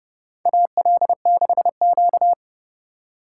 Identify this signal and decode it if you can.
Morse code, heard as a keyed tone, about 30 wpm, 700 Hz. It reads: AL6Q
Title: USA Call Signs - 4 Characters - 3x Character Spacing - 30wpm